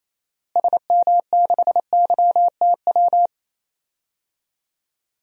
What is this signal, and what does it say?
Morse code, heard as a keyed tone, about 28 wpm, 700 Hz. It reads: SM6YTW
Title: DXCC-100 Call Signs - 6 Characters - Repeated 2x - 28wpm